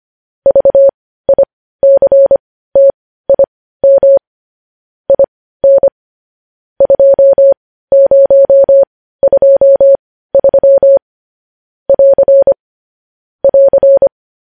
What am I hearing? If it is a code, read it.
Morse code, heard as a keyed tone, about 25 wpm, spaced as for 15 wpm, 550 Hz. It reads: VICTIM IN 2023 <AR> <AR>
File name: podcast / News headlines-1x-25x15